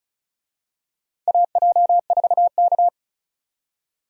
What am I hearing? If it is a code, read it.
Morse code, heard as a keyed tone, about 36 wpm, 700 Hz. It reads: AJ4K
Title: USA Call Signs - 4 Characters - 35wpm